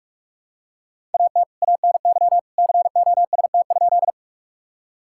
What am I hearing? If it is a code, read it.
Morse code, heard as a keyed tone, about 46 wpm, 700 Hz. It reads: AT ANY COST?
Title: Sentences from Top 1000 Words - Review Part 5 - 45wpm